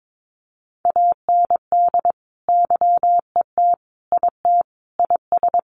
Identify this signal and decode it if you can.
Morse code, heard as a keyed tone, about 22 wpm, 700 Hz. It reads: AND YET IT IS
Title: Sentences from Top 1500 Words 22wpm